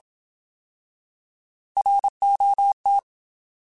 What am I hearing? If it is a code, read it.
Morse code, heard as a keyed tone, about 27 wpm, 800 Hz. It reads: ROT
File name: Alphabetical / Alphabetical mp3s / Rs @25